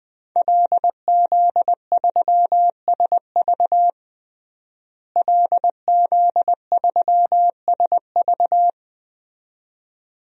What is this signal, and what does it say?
Morse code, heard as a keyed tone, about 20 wpm, 700 Hz. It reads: LZ3SV LZ3SV
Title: DXCC-100 Call Signs - 5 Characters - Repeated 2x - 20wpm